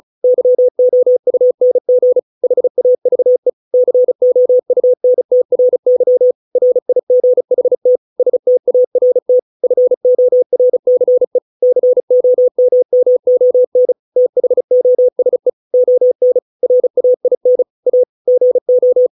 Morse code, heard as a keyed tone, about 34 wpm, 500 Hz. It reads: YOUNG HAVE COUNTRY RIGHT START FORCE COMMON THOSE ON RAIN A GO
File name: 632 words / 632 Words 35WPM